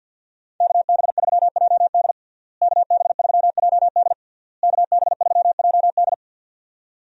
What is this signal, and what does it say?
Morse code, heard as a keyed tone, about 46 wpm, 700 Hz. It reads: KB3JD KB3JD KB3JD
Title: USA Call Signs - 5 Characters - Repeated 3x - 50wpm